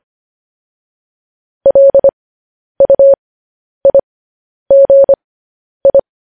Morse code, heard as a keyed tone, about 25 wpm, spaced as for 10 wpm, 550 Hz. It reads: LUIGI
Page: news headlines in morse code at 10 wpm